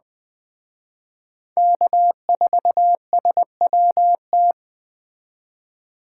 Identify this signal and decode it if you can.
Morse code, heard as a keyed tone, about 21 wpm, 700 Hz. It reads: K4SWT